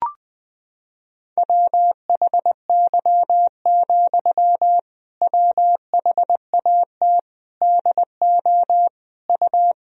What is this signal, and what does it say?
Morse code, heard as a keyed tone, about 21 wpm, 700 Hz. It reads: WHY, WHAT DO U
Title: Sentences from Top 200 Words and Abbreviations 20wpm